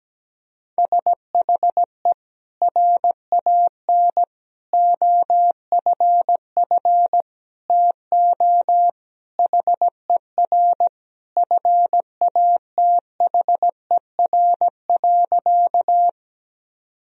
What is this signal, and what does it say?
Morse code, heard as a keyed tone, about 17 wpm, 700 Hz. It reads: SHE RAN OFF TO HER FATHER.
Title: Sentences from Top 1000 Words - Review Part 3 - 17wpm